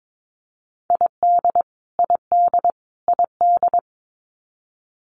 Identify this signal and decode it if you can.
Morse code, heard as a keyed tone, about 22 wpm, 700 Hz. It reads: ID ID ID
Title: US States Repeated 3x 22wpm